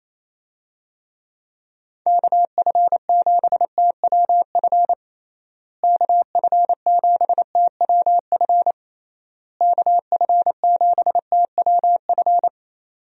Morse code, heard as a keyed tone, about 28 wpm, 700 Hz. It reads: KF7TWF KF7TWF KF7TWF
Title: USA Call Signs - 6 Characters - Repeated 3x - 28wpm